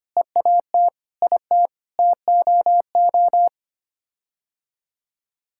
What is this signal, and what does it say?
Morse code, heard as a keyed tone, about 25 wpm, 700 Hz. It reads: EAT IT TOO
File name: Sets of 9 Words - 25wpm